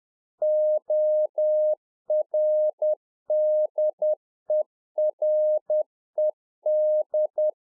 Morse code, heard as a keyed tone, about 10 wpm, 610 Hz. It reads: ORDERED